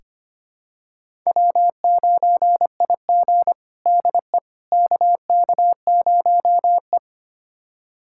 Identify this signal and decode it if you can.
Morse code, heard as a keyed tone, about 25 wpm, 700 Hz. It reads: W9IG DE KK0E
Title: USA Callsign DE Callsign - 4 Characters - 25wpm